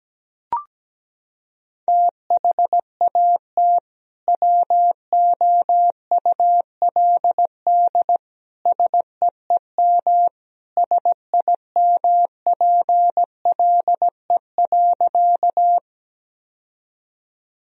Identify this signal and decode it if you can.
Morse code, heard as a keyed tone, about 17 wpm, 700 Hz. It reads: THAT WOULD SEEM SIMPLE.
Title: Sentences from Top 1000 Words - Review Part 1 - 17wpm